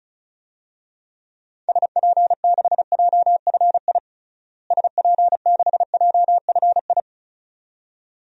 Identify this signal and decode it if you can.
Morse code, heard as a keyed tone, about 34 wpm, 700 Hz. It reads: SP6JFI SP6JFI
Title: DXCC-100 Call Signs - 6 Characters - Repeated 2x - 35wpm